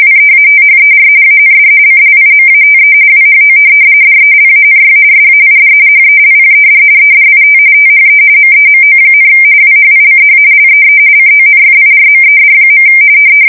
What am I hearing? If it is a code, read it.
RTTY, about 45.45 baud. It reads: WEATHER AND CLIMATE WHO DELANY SAID HELPED EDIT THE MEMO, WHICH LEE CONFIRMED.
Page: 170 Shift 2125/2295